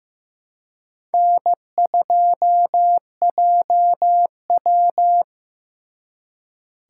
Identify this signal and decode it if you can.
Morse code, heard as a keyed tone, about 18 wpm, 700 Hz. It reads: N2JW
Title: Speed Racing - USA 1x2 Call Signs 15wpm